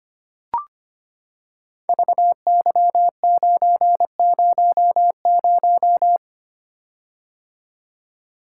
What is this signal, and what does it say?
Morse code, heard as a keyed tone, about 25 wpm, 700 Hz. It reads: VY900
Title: Code Groups - 5 Letter:Numbers - 25wpm